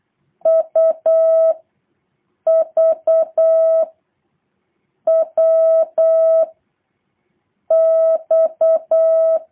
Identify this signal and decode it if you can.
Morse code, heard as a keyed tone, about 8 wpm, 650 Hz. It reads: U V W X
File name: jh8cbh / 2023cw